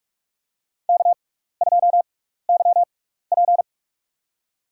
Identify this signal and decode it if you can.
Morse code, heard as a keyed tone, about 46 wpm, spaced as for 16 wpm, 700 Hz. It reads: K2YP
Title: USA Call Signs - 6x Character Spacing - 45wpm